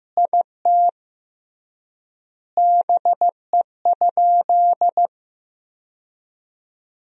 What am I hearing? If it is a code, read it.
Morse code, heard as a keyed tone, about 15 wpm, 700 Hz. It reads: IT BE?
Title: 3x Word-Spacing - Sentences from Top 100 Words - 15wpm